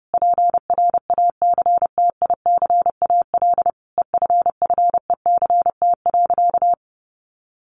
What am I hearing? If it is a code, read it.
Morse code, heard as a keyed tone, about 30 wpm, 700 Hz. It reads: PRACTICAL EFFECT.
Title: Sentences from Top 1400 Words 30wpm